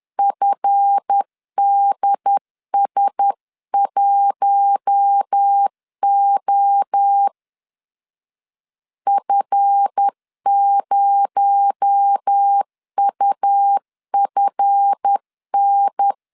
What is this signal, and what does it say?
Morse code, heard as a keyed tone, about 10 wpm, 800 Hz. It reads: FDS1O F0UFN
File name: LEZIONI / 7 GF9Z / 7 GF 10wpm 2514